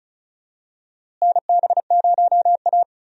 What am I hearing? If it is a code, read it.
Morse code, heard as a keyed tone, about 36 wpm, 700 Hz. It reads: NB0A
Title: USA Call Signs - 4 Characters - Repeated 3x - 35wpm